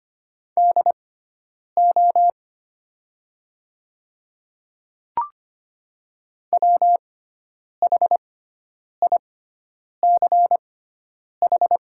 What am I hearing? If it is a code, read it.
Morse code, heard as a keyed tone, about 25 wpm, spaced as for 9 wpm, 700 Hz. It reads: DO WHICH
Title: 6x Character Spacing - Top 100 Words - 25wpm